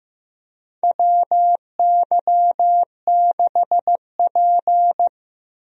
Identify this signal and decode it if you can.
Morse code, heard as a keyed tone, about 15 wpm, 700 Hz. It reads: WY6P
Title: USA Call Signs - 4 Characters - 15wpm